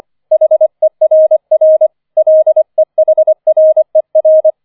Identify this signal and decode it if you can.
Morse code, heard as a keyed tone, about 25 wpm, 630 Hz. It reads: HERR LEHRER